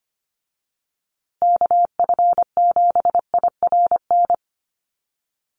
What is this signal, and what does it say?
Morse code, heard as a keyed tone, about 25 wpm, 700 Hz. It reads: KF7IRN
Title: USA Call Signs Repeated 2x 25wpm